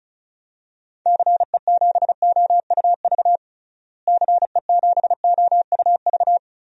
Morse code, heard as a keyed tone, about 36 wpm, 700 Hz. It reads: CE7OUV CE7OUV
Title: DXCC-100 Call Signs - 6 Characters - Repeated 3x - 35wpm